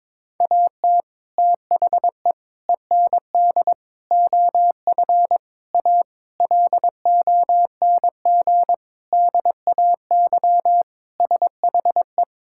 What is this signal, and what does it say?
Morse code, heard as a keyed tone, about 22 wpm, 700 Hz. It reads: AT THE END OF A LONG DAY SHE
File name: practice / 22wpm / Sentences with 4-Word Phrases - 22wpm